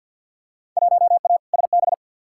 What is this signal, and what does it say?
Morse code, heard as a keyed tone, about 47 wpm, 700 Hz. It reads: 1A SB
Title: Field Day Exchanges Encore - 50wpm